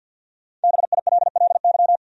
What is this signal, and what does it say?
Morse code, heard as a keyed tone, about 48 wpm, 700 Hz. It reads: BILLY
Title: USA Common Names - 50wpm